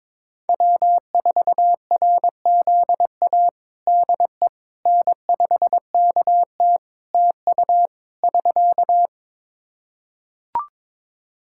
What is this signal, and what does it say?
Morse code, heard as a keyed tone, about 22 wpm, 700 Hz. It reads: W4RZA DE N5KT TU <SK>
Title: Practice QSO - Exchange 4 - Ending - 22wpm